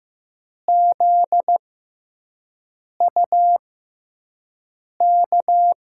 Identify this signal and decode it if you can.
Morse code, heard as a keyed tone, about 15 wpm, 700 Hz. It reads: Z U K
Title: USA Call Signs - 6x Character Spacing - 15wpm